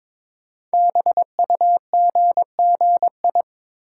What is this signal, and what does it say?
Morse code, heard as a keyed tone, about 22 wpm, 700 Hz. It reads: BUGGI
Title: Code Groups - 5 Letter:Numbers - 22wpm